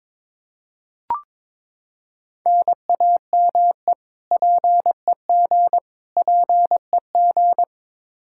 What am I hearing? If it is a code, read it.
Morse code, heard as a keyed tone, about 22 wpm, 700 Hz. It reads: NAME PEG PEG
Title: Practice QSO - Exchange 1 - Beginning - 22wpm